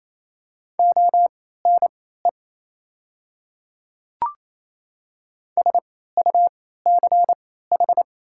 Morse code, heard as a keyed tone, about 27 wpm, spaced as for 16 wpm, 700 Hz. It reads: ONE SUCH
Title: 3x Character Spacing - Top 200 Words - 28wpm